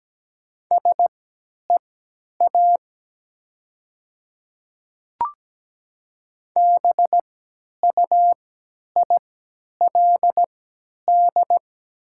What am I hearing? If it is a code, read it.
Morse code, heard as a keyed tone, about 17 wpm, spaced as for 10 wpm, 700 Hz. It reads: SEA BUILD
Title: 3x Character Spacing - Top 300 Words - 17wpm